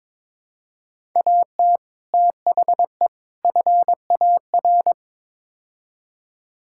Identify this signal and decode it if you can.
Morse code, heard as a keyed tone, about 22 wpm, 700 Hz. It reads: AT THE FAR
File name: practice / 22wpm / Build A Phrase - 5-Word Phrases - 22wpm